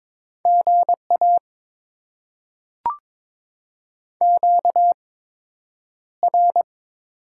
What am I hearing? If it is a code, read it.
Morse code, heard as a keyed tone, about 22 wpm, spaced as for 9 wpm, 700 Hz. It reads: GA QR
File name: Top 20 QSO Elements - 8x Character Spacing - 22wpm